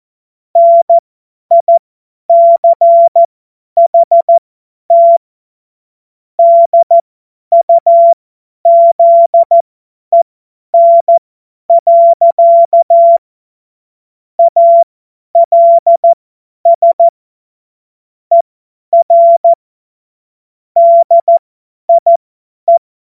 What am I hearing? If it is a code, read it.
Morse code, heard as a keyed tone, about 14 wpm, spaced as for 10 wpm, 680 Hz. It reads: NICHT DUZEN. ALS ER DIE
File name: Morsekurs / Witze / Witz 13. September 2022 50 F70